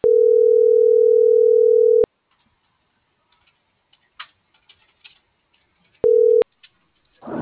Ringback tone (North America), 440 + 480 Hz.